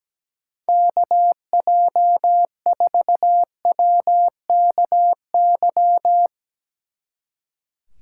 Morse code, heard as a keyed tone, about 18 wpm, 700 Hz. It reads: KJ4WKY
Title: Speed Racing - USA 2x3 Call Signs 17wpm